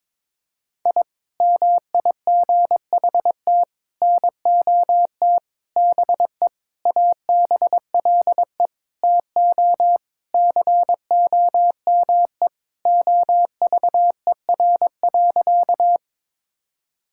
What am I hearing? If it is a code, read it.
Morse code, heard as a keyed tone, about 22 wpm, 700 Hz. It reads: I MIGHT NOT BE ABLE TO COME OVER.
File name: practice / 22wpm / Sentences with 4-Word Phrases - 22wpm